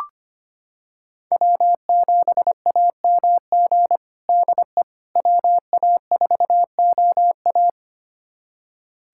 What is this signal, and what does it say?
Morse code, heard as a keyed tone, about 25 wpm, 700 Hz. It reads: W7AMG DE WA4OA
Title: USA Callsign DE Callsign - 5 Characters - 25wpm